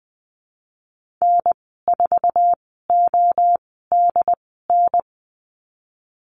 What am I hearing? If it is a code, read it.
Morse code, heard as a keyed tone, about 19 wpm, spaced as for 14 wpm, 700 Hz. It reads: N4ODN
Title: 069 Lesson 12g - Letter d - Call Signs 20wpm